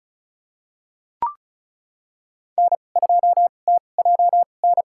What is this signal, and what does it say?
Morse code, heard as a keyed tone, about 34 wpm, spaced as for 25 wpm, 700 Hz. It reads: N2TJN
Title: USA Call Signs - 5 Characters - 2x Character Spacing - 35wpm